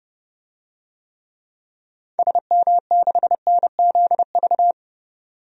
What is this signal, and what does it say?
Morse code, heard as a keyed tone, about 30 wpm, 700 Hz. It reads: SM6NZV
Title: DXCC-100 Call Signs - 6 Characters - 30wpm